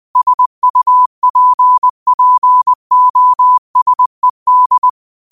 Morse code, heard as a keyed tone, about 20 wpm, 1000 Hz. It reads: SUPPOSED
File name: 20wpm